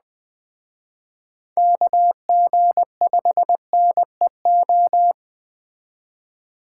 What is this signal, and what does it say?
Morse code, heard as a keyed tone, about 20 wpm, 700 Hz. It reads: KG5NEO
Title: Speed Racing - USA 2x3 Call Signs 17wpm